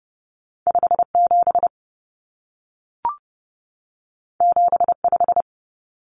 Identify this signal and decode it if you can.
Morse code, heard as a keyed tone, about 30 wpm, 700 Hz. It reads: 57 75
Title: Common Cut Numbers - 2-Digit 30wpm